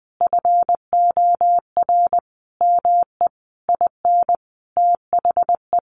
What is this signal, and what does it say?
Morse code, heard as a keyed tone, about 19 wpm, 700 Hz. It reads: FOR ME IN THE
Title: Sentences from Top 1500 Words 20wpm